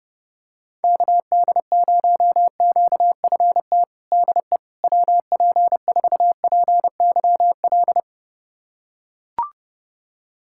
Morse code, heard as a keyed tone, about 32 wpm, 700 Hz. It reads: KD0QFT DE WP4PYL
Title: USA Callsign DE Callsign - 6 Characters - 30wpm